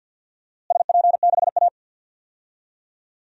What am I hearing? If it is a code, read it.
Morse code, heard as a keyed tone, about 48 wpm, 700 Hz. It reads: SP6A
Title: DXCC-100 Call Signs - 4 Characters - Repeated 2x - 50wpm